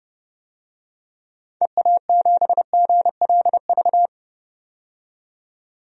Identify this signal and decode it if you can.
Morse code, heard as a keyed tone, about 30 wpm, 700 Hz. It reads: EA7GLV
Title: DXCC-100 Call Signs - 6 Characters - 30wpm